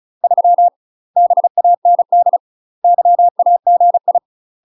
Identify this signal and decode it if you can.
Morse code, heard as a keyed tone, about 36 wpm, 700 Hz. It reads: 3 BAND YAGI
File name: UK MASTER QSOs TRAINING / 35WPM / UK MASTER QSOs 0026